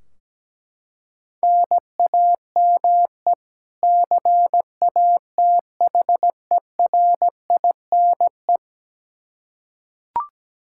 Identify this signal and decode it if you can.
Morse code, heard as a keyed tone, about 17 wpm, 700 Hz. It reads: NAME CATHERINE
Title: Practice QSO - Exchange 1 - Beginning - 17wpm